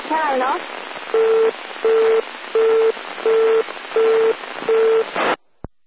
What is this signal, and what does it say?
Busy tone (China, 450 Hz), with speech.